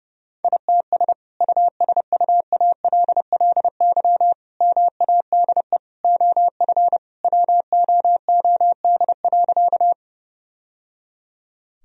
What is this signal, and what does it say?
Morse code, heard as a keyed tone, about 30 wpm, 700 Hz. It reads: ITS USUALLY MADE OF WOOD.
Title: Sentences from Top 1000 Words - Review Part 3 - 30wpm